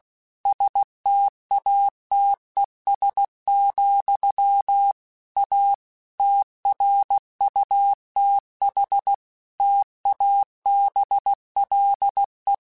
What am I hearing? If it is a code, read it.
Morse code, heard as a keyed tone, about 16 wpm, 800 Hz. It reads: STATES, A TRUTH TABLE
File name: Technical Subjects / Technical Subjects-mp3s / AND Gate @15